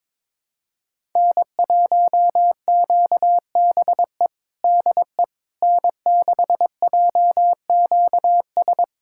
Morse code, heard as a keyed tone, about 22 wpm, 700 Hz. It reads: N1QBE DE N6JQS
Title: USA Callsign DE Callsign - 5 Characters - 22wpm